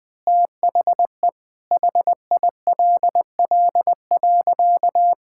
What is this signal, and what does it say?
Morse code, heard as a keyed tone, about 20 wpm, 700 Hz. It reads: THE HILL.
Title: Sentences with 6-Word Phrases - 20wpm